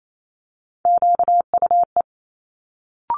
Morse code, heard as a keyed tone, about 28 wpm, 700 Hz. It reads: QUE
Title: Multi-letter Phonemes Quick Fire 28wpm